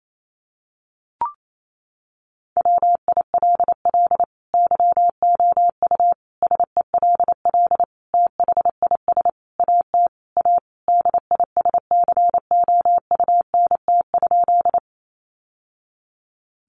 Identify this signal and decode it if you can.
Morse code, heard as a keyed tone, about 28 wpm, 700 Hz. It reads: WILL YOU SELL THIS AT A DISCOUNT?
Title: Sentences from Top 1200 Words 28wpm